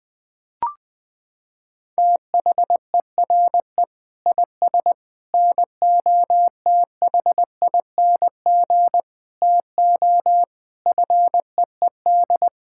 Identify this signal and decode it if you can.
Morse code, heard as a keyed tone, about 21 wpm, 700 Hz. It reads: THERE IS NOTHING TO FEED
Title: Sentences from Top 1600 Words 20wpm